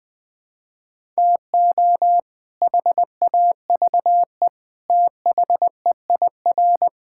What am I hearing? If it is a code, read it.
Morse code, heard as a keyed tone, about 20 wpm, 700 Hz. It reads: TO HAVE THEIR
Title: Sets of 8 Words - 20wpm